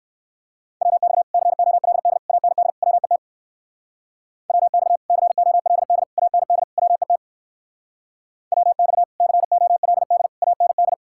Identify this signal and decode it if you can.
Morse code, heard as a keyed tone, about 60 wpm, 700 Hz. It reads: WX COLD AND WET WX COLD AND WET WX COLD AND